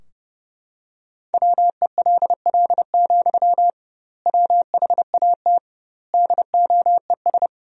Morse code, heard as a keyed tone, about 30 wpm, 700 Hz. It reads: WELL, WHAT DOES
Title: 2x Word-Spacing - Sentences from Top 500 Words - 30wpm